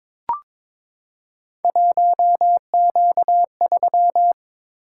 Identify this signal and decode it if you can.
Morse code, heard as a keyed tone, about 22 wpm, 700 Hz. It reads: 1Q3
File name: practice / 22wpm / Three Letter-Number Combinations - 22wpm